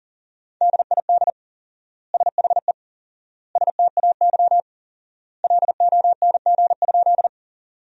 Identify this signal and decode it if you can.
Morse code, heard as a keyed tone, about 38 wpm, 700 Hz. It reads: DID SHE STAY LONG?
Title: 4x Word-Spacing - Sentences from Top 500 Words - 40wpm